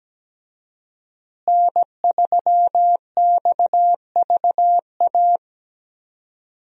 Morse code, heard as a keyed tone, about 17 wpm, 700 Hz. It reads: N3XVA
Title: USA Call Signs - 5 Characters - Repeated 3x - 17wpm